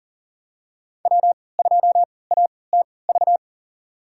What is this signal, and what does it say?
Morse code, heard as a keyed tone, about 40 wpm, spaced as for 23 wpm, 700 Hz. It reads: W2ATV
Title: USA Call Signs - 3x Character Spacing - 40wpm